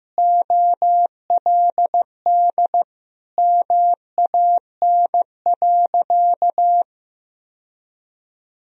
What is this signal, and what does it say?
Morse code, heard as a keyed tone, about 14 wpm, 700 Hz. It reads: OLD MAN.
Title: Sentences from Top 1000 Words - Review Part 4 - 15wpm